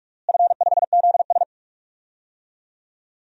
Morse code, heard as a keyed tone, about 44 wpm, 700 Hz. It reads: F5ZS
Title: DXCC-100 Call Signs - 4 Characters - Repeated 2x - 45wpm